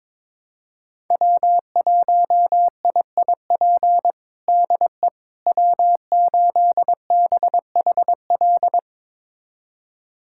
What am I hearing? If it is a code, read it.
Morse code, heard as a keyed tone, about 22 wpm, 700 Hz. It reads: W1IIP DE W8BHL
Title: USA Callsign DE Callsign - 5 Characters - 22wpm